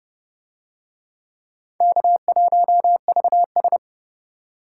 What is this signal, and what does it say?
Morse code, heard as a keyed tone, about 30 wpm, 700 Hz. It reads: K1VS